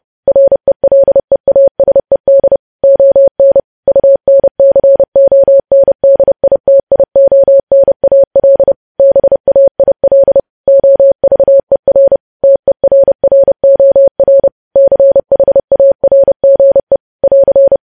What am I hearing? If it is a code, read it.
Morse code, heard as a keyed tone, about 30 wpm, 550 Hz. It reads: RELEASED ON UNCONDITIONAL BAIL OVER TERROR CHARGE <AR>
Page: news headlines in morse code at 30 wpm